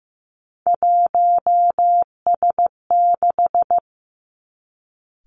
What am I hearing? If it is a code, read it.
Morse code, heard as a keyed tone, about 15 wpm, 700 Hz. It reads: 1S6
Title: Three Challenging Letter-Number Combinations 15wpm